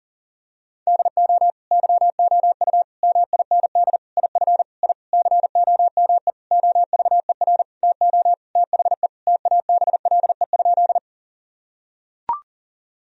Morse code, heard as a keyed tone, about 40 wpm, 700 Hz. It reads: DO YOU MIND IF I COME OVER TO THE TABLE?
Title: Sentences from Top 1000 Words - Review Part 2 - 40wpm